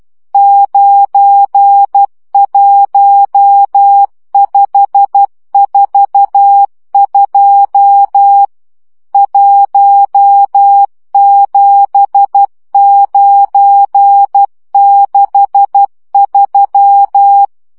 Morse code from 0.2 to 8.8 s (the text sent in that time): 91542